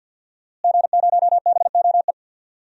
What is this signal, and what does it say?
Morse code, heard as a keyed tone, about 46 wpm, 700 Hz. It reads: G0BOE
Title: DXCC-100 Call Signs - 5 Characters - Repeated 2x - 50wpm